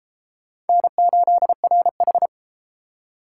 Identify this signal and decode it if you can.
Morse code, heard as a keyed tone, about 28 wpm, 700 Hz. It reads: N8RH